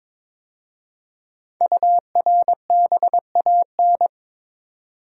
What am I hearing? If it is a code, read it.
Morse code, heard as a keyed tone, about 22 wpm, 700 Hz. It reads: URBAN
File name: 5 Letter Words - First 3 Letters Given - 22wpm